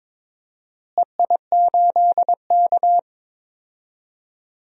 Morse code, heard as a keyed tone, about 21 wpm, 700 Hz. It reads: EI8K
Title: DXCC-100 Call Signs - 4 Characters - Repeated 3x - 22wpm